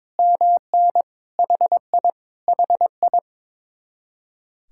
Morse code, heard as a keyed tone, about 22 wpm, 700 Hz. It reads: MN HI HI
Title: Speed-Racing Common QSO Phrases - 22wpm